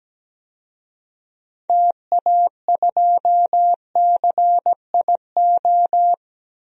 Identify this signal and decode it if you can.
Morse code, heard as a keyed tone, about 17 wpm, 700 Hz. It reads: TA2CIO
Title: DXCC-100 Call Signs - 6 Characters - 17wpm